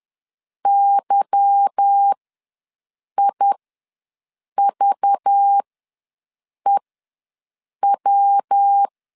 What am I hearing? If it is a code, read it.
Morse code, heard as a keyed tone, about 10 wpm, spaced as for 6 wpm, 800 Hz. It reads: YIVEW
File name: LEZIONI / 10 SIGNS / 10A SLASH ws 10wpm 4121